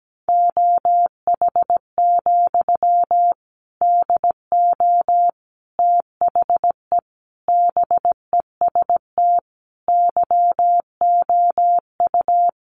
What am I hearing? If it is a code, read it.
Morse code, heard as a keyed tone, about 17 wpm, 700 Hz. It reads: OH, DO THE BEST YOU
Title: Sentences from Top 500 Words 17wpm